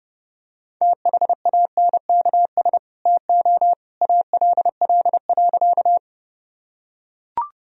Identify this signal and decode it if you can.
Morse code, heard as a keyed tone, about 30 wpm, 700 Hz. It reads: THANKS TO ALL.
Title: Sentences from Top 1000 Words - Review Part 3 - 30wpm